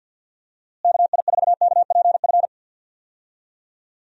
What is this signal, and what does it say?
Morse code, heard as a keyed tone, about 52 wpm, 700 Hz. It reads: KI4KPF
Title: USA Call Signs - 6 Characters - Repeated 3x - 50wpm